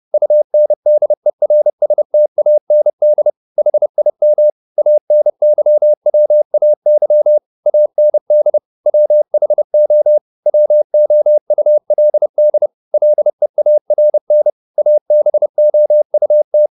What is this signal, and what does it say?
Morse code, heard as a keyed tone, about 30 wpm, 600 Hz. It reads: UNDERSTAND HIM ANYWAY AND WHO WOULD LEARN ABOUT
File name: the old man and the sea / the old man and the sea 1 30wpm